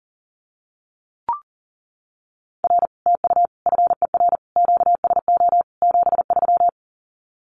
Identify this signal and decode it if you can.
Morse code, heard as a keyed tone, about 40 wpm, 700 Hz. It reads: R TU FER QSO 73
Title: Common QSO Phrases Encore 40wpm